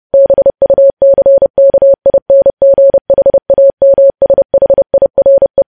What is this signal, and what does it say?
Morse code, heard as a keyed tone, about 30 wpm, 550 Hz. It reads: BUCKINGHAMSHIRE
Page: news headlines in morse code at 30 wpm